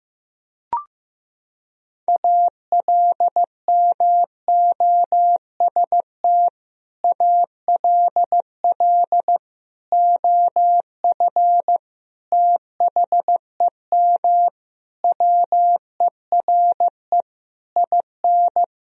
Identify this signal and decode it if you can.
Morse code, heard as a keyed tone, about 15 wpm, 700 Hz. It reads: ALMOST ALL OF THEM WERE IN
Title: Sentences from Top 1000 Words - Review Part 4 - 15wpm